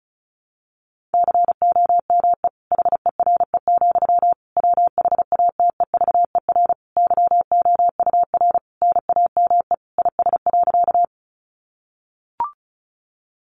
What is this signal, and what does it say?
Morse code, heard as a keyed tone, about 34 wpm, 700 Hz. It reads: COME HERE, WHATEVER YOUR NAME IS.
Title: Sentences from Top 1100 Words 35wpm